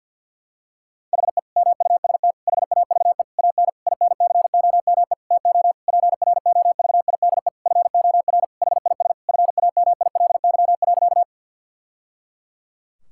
Morse code, heard as a keyed tone, about 50 wpm, 700 Hz. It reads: HE MUST HAVE AN INCOME TO PROVIDE FOR HIS FAMILY.
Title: Sentences from Top 1000 Words - Review Part 5 - 50wpm